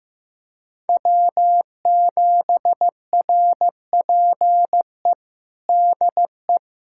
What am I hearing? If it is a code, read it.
Morse code, heard as a keyed tone, about 15 wpm, 700 Hz. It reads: W7RPE DE
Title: USA Callsign DE Callsign - 5 Characters - 15wpm